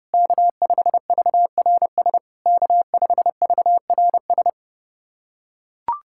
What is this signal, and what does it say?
Morse code, heard as a keyed tone, about 31 wpm, 690 Hz. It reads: K5VRS K5VRS
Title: SOTA QSO Practice 30wpm